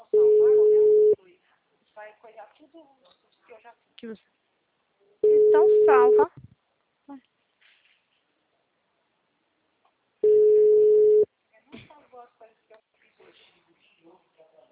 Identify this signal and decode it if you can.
Ringback tone (Europe, 425 Hz), with speech.
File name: out- -2002- - - .40132